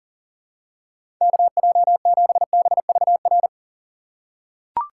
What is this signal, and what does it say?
Morse code, heard as a keyed tone, about 40 wpm, 700 Hz. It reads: KJ7BVR
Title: USA Call Signs - 6 Characters - 6x Character Spacing - 40wpm